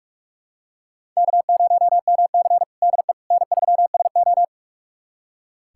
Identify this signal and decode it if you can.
Morse code, heard as a keyed tone, about 44 wpm, 690 Hz. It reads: K0MC DE N3SO
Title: USA Callsign DE Callsign - 4 Characters - 45wpm